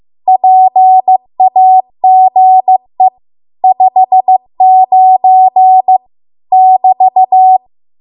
Morse code, heard as a keyed tone, about 15 wpm, 750 Hz. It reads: PAGE 59 <BT>
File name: Archive / 15 WPM / 15WPM